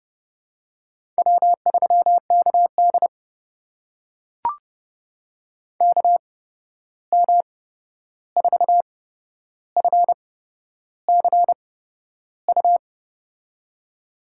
Morse code, heard as a keyed tone, about 31 wpm, spaced as for 11 wpm, 700 Hz. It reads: W3KD KM4FCU
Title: USA Call Signs - 8x Character Spacing - 30wpm